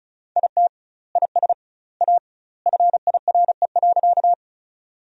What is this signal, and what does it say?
Morse code, heard as a keyed tone, about 34 wpm, 700 Hz. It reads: IT IS A FIRE.
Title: 2x Word-Spacing - Sentences from Top 400 Words - 35wpm